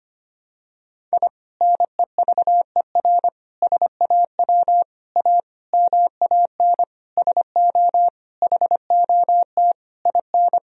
Morse code, heard as a keyed tone, about 25 wpm, 700 Hz. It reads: I NEVER SAW A MAN SO HOT IN
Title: Sentences from Top 1000 Words - Review Part 3 - 25wpm